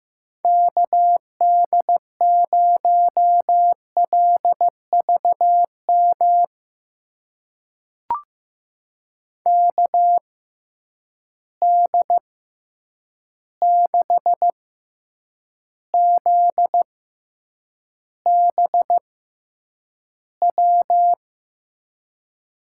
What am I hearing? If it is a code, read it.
Morse code, heard as a keyed tone, about 16 wpm, spaced as for 8 wpm, 700 Hz. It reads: KD0LVM KD6ZBW
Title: USA Call Signs - 6 Characters - 6x Character Spacing - 15wpm